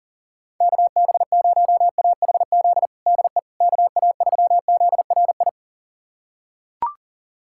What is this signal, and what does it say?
Morse code, heard as a keyed tone, about 40 wpm, 700 Hz. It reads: KB0AHZ DE KA3ZRI
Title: USA Callsign DE Callsign - 6 Characters - 40wpm